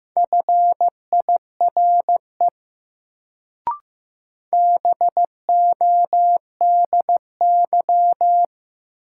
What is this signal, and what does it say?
Morse code, heard as a keyed tone, about 15 wpm, 700 Hz. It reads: FIRE BODY
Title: Sentences from Top 400 Words - New Word Drill - 15wpm